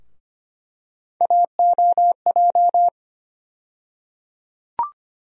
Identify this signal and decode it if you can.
Morse code, heard as a keyed tone, about 25 wpm, 700 Hz. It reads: AOJ
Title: USA Call Sign Postfixes - 3 Letters 25wpm